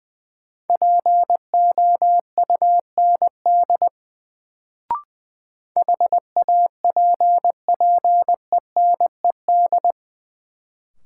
Morse code, heard as a keyed tone, about 20 wpm, 700 Hz. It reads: POUND HAPPENED
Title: Sentences from Top 1000 Words - Review Part 2 - New Word Drill - 20wpm